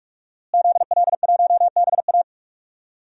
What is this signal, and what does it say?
Morse code, heard as a keyed tone, about 46 wpm, 700 Hz. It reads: ZL1BA
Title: DXCC-100 Call Signs - 5 Characters - Repeated 2x - 45wpm